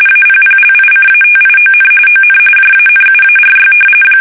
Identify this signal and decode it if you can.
RTTY, about 45.45 baud. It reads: APPLICATION ON JANUARY
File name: fsk850 / battleshipnj